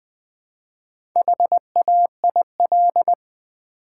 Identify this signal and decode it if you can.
Morse code, heard as a keyed tone, about 20 wpm, 700 Hz. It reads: HAIL
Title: Build A Word - 2 Letters At A Time - 4-Letter Words - 20wpm